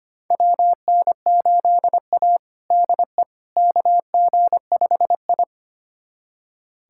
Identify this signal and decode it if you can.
Morse code, heard as a keyed tone, about 25 wpm, 700 Hz. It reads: WN8A DE KG5I
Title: USA Callsign DE Callsign - 4 Characters - 25wpm